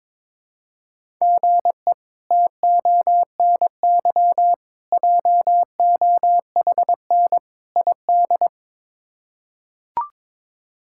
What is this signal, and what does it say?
Morse code, heard as a keyed tone, about 21 wpm, 700 Hz. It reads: GE TONY JOHN ID
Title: K1USN SST QSO Practice - 22wpm